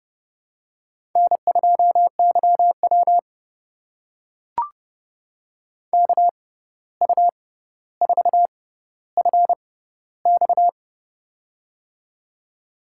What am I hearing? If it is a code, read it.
Morse code, heard as a keyed tone, about 30 wpm, spaced as for 14 wpm, 700 Hz. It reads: N2YW KU4FX
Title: USA Call Signs - 6x Character Spacing - 30wpm